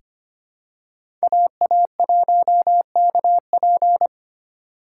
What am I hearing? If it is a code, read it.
Morse code, heard as a keyed tone, about 25 wpm, 700 Hz. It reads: AA1KP
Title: USA Call Signs - 5 Characters - 2x Character Spacing - 25wpm